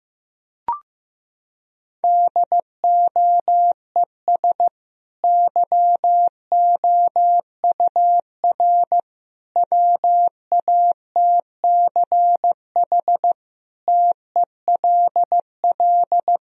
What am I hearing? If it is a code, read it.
Morse code, heard as a keyed tone, about 15 wpm, 700 Hz. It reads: DOES YOUR WATCH TELL
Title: Sentences from Top 1000 Words - Review Part 1 - 15wpm